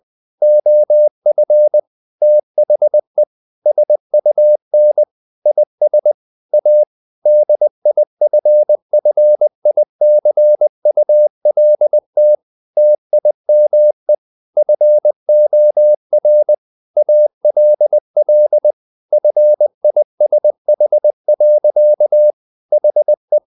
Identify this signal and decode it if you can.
Morse code, heard as a keyed tone, about 20 wpm, 600 Hz. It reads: OF THE SUN IS A DIFFICULT TIME FOR ALL FISH. HE
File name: the old man and the sea part 2 20wpm